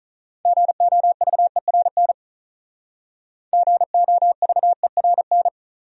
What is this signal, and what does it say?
Morse code, heard as a keyed tone, about 39 wpm, 700 Hz. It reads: GOVERN GOVERN
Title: Speed-Racing Top 500 Words 35wpm